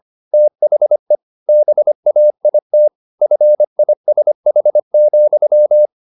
Morse code, heard as a keyed tone, about 25 wpm, 600 Hz. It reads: THE BAIT FISH,
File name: the old man and the sea part 1 25wpm